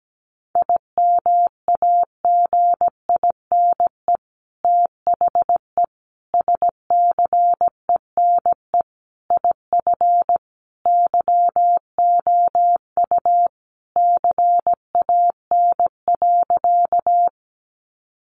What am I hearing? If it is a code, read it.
Morse code, heard as a keyed tone, about 17 wpm, 700 Hz. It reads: IMAGINE THE SCENE IF YOU CAN.
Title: Sentences from Top 1400 Words 17wpm